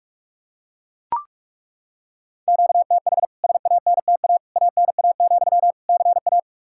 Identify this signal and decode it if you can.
Morse code, heard as a keyed tone, about 41 wpm, 700 Hz. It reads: QTH SANTA ANA, CA
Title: K1USN SST QSO Practice 45wpm